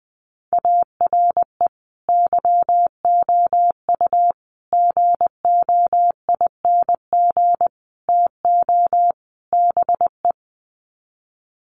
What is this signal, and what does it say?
Morse code, heard as a keyed tone, about 20 wpm, 700 Hz. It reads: ARE YOU GOING TO BE
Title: Sets of 5 Words Encore 20wpm